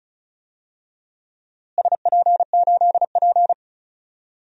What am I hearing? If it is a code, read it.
Morse code, heard as a keyed tone, about 36 wpm, 700 Hz. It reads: SP8P
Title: DXCC-100 Call Signs - 4 Characters - Repeated 2x - 35wpm